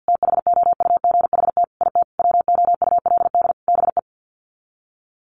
Morse code, heard as a keyed tone, about 50 wpm, 700 Hz. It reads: THOUGHT IT WOULD BE
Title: Sets of 5 Words Encore 50wpm